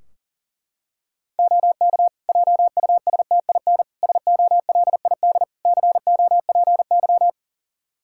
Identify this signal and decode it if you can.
Morse code, heard as a keyed tone, about 37 wpm, 700 Hz. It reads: OK JUSTIN SOLID COPY
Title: Practice QSO - Exchange 3 - More Info - 40wpm